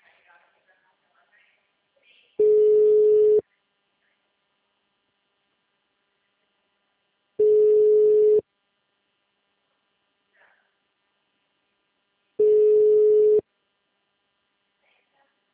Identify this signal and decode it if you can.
Ringback tone (Europe, 425 Hz), with speech.